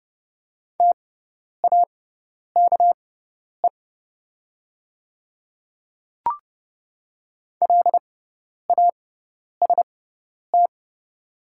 Morse code, heard as a keyed tone, about 30 wpm, spaced as for 10 wpm, 690 Hz. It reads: TAKE LAST
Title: 6x Character Spacing - Top 500 Words - 30wpm